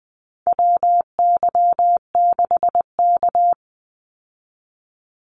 Morse code, heard as a keyed tone, about 20 wpm, 700 Hz. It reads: WY6K
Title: Call Signs - 20wpm